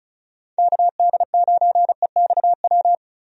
Morse code, heard as a keyed tone, about 36 wpm, 700 Hz. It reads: KD9EXW
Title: USA Call Signs - 6 Characters - 35wpm